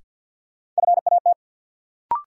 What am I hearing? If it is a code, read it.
Morse code, heard as a keyed tone, about 52 wpm, 700 Hz. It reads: FAT
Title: Sentences from Top 600 Words - New Word Drill - 50wpm